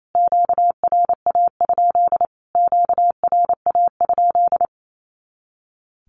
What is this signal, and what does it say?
Morse code, heard as a keyed tone, about 28 wpm, 700 Hz. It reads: QRA? QRA?
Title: Q-Codes Repeated 3x 28wpm